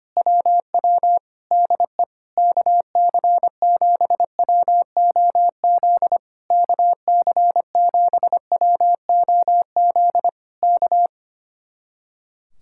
Morse code, heard as a keyed tone, about 24 wpm, 700 Hz. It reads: WW DE KC7WOZ KC7WOZ K
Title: Speed-Racing Common QSO Phrases - 25wpm